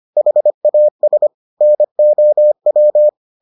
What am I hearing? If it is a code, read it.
Morse code, heard as a keyed tone, about 25 wpm, 600 Hz. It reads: HAS NOW